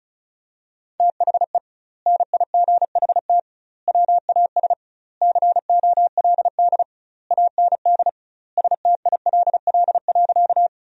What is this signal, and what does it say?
Morse code, heard as a keyed tone, about 34 wpm, 700 Hz. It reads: THE NIGHT WAS COLD AND STILL.
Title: 2x Word-Spacing - Sentences from Top 400 Words - 35wpm